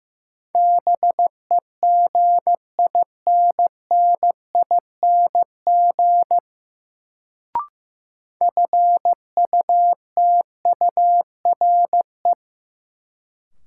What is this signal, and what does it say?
Morse code, heard as a keyed tone, about 15 wpm, 700 Hz. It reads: BEGINNING FUTURE
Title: Sets of 3 Words - New Word Drill - 15wpm